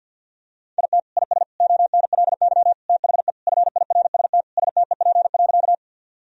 Morse code, heard as a keyed tone, about 50 wpm, 700 Hz. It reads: IT IS ONLY THE FIRST STEP.
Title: Sentences from Top 1000 Words - Review Part 2 - 50wpm